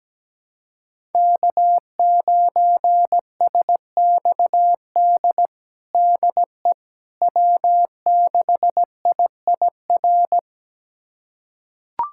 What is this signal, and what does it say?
Morse code, heard as a keyed tone, about 17 wpm, 700 Hz. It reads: K9SXD DE W6IIR
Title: USA Callsign DE Callsign - 5 Characters - 17wpm